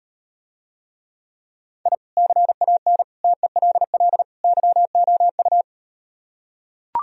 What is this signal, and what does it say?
Morse code, heard as a keyed tone, about 38 wpm, 700 Hz. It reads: I CAN TELL YOU